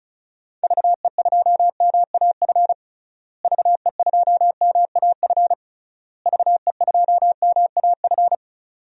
Morse code, heard as a keyed tone, about 36 wpm, 700 Hz. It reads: VE2MAF VE2MAF VE2MAF
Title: DXCC-100 Call Signs - 6 Characters - Repeated 3x - 35wpm